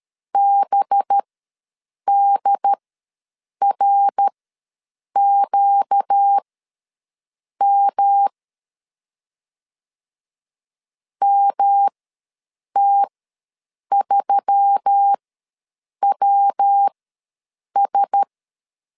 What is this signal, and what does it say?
Morse code, heard as a keyed tone, about 13 wpm, spaced as for 7 wpm, 800 Hz. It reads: BDRQM MT3WS
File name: LEZIONI / 5 LBVC / 5 LBVC ws 12wpm 3159